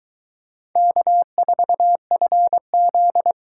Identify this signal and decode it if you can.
Morse code, heard as a keyed tone, about 20 wpm, 700 Hz. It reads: K4FZ